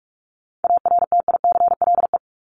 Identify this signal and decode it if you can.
Morse code, heard as a keyed tone, about 44 wpm, 700 Hz. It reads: ARTICLE
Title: Top 751-1000 Words 45wpm